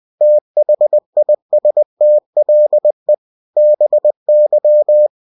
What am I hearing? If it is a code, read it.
Morse code, heard as a keyed tone, about 20 wpm, 600 Hz. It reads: THISTLE BY
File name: cw-books / winnie-the-pooh / winnie-20wpm-page-0084